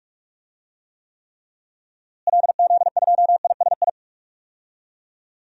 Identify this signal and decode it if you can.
Morse code, heard as a keyed tone, about 41 wpm, 700 Hz. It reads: LZ2ISI
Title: DXCC-100 Call Signs - 6 Characters - Repeated 3x - 45wpm